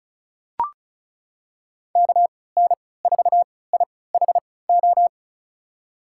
Morse code, heard as a keyed tone, about 36 wpm, spaced as for 20 wpm, 700 Hz. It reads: KN4IHO
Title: USA Call Signs - 6 Characters - 3x Character Spacing - 35wpm